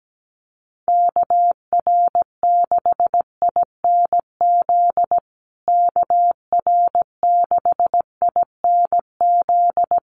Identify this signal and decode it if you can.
Morse code, heard as a keyed tone, about 17 wpm, 700 Hz. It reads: KR6INZ KR6INZ
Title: USA Call Signs Repeated 2x 17wpm